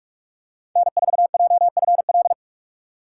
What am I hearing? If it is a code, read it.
Morse code, heard as a keyed tone, about 48 wpm, 700 Hz. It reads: N4JFL
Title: USA Call Signs - 5 Characters - 6x Character Spacing - 45wpm